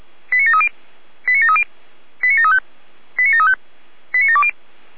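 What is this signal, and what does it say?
Five-tone selective call (ZVEI), heard as quick runs of five tones, tones of about 70 ms. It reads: 89730 89730 89735 89735 89720
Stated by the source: Received on SDR as ZVEI five-tone selective call.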